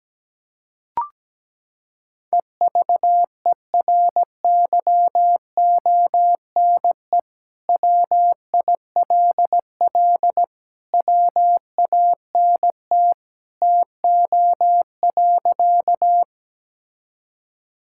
Morse code, heard as a keyed tone, about 17 wpm, 700 Hz. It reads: EVERYONE WILL WANT TO.
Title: Sentences from Top 1000 Words - Review Part 5 - 17wpm